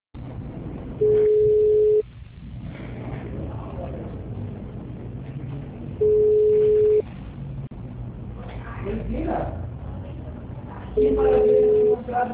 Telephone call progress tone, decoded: ringback tone

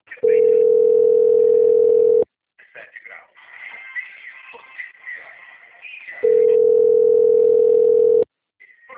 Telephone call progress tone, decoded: ringback tone

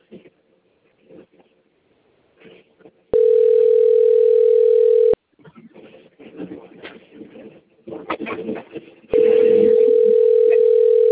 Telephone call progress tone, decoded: ringback tone

